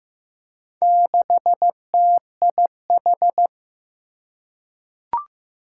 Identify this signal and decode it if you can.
Morse code, heard as a keyed tone, about 15 wpm, 700 Hz. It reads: 6TIH
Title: Four Challenging Letter-Number Combinations 15wpm